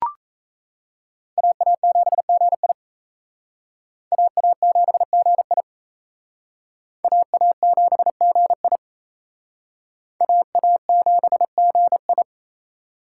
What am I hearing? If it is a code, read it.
Morse code, heard as a keyed tone, about 35 wpm, 700 Hz. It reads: AA7GI AA7GI AA7GI AA7GI
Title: Speed Racing - USA Call Signs 28wpm